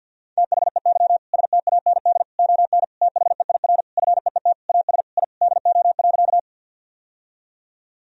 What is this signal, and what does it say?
Morse code, heard as a keyed tone, about 52 wpm, 700 Hz. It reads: THEY STAND ON THEIR FEET AS I DO.
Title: Sentences from Top 1000 Words - Review Part 1 - 50wpm